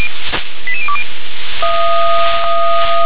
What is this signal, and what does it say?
Five-tone selective call (ZVEI), heard as quick runs of five tones, tones of about 80 ms. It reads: 88020
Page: /357/SDR/ZVEI/